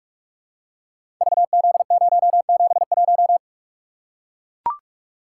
Morse code, heard as a keyed tone, about 44 wpm, 690 Hz. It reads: VZ071
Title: Code Groups - 5 Letter:Numbers - 45wpm